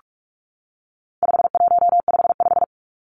Morse code, heard as a keyed tone, about 44 wpm, 700 Hz. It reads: 5155
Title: Numbers - 4-Digit 45wpm